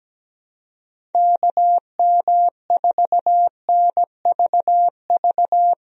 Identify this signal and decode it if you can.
Morse code, heard as a keyed tone, about 17 wpm, 700 Hz. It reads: KM4NVV
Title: USA Call Signs - 6 Characters - Repeated 2x - 17wpm